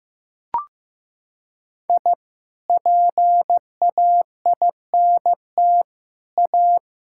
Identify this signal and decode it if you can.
Morse code, heard as a keyed tone, about 15 wpm, 700 Hz. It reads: I PAINT A
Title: Sentences from Top 1000 Words - Review Part 6 - 15wpm